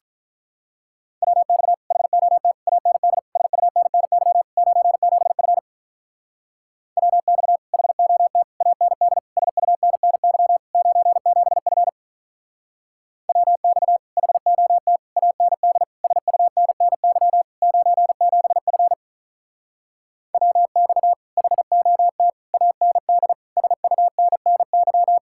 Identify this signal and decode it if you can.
Morse code, heard as a keyed tone, about 43 wpm, 700 Hz. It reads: WX HOT AND SUNNY 97F WX HOT AND SUNNY 97F WX HOT AND SUNNY 97F WX HOT AND SUNNY
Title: Speed-Racing Common QSO Phrases - 35wpm